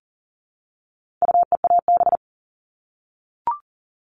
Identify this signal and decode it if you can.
Morse code, heard as a keyed tone, about 40 wpm, 700 Hz. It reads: UEAB
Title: All Cut Numbers - 4-Digit 40wpm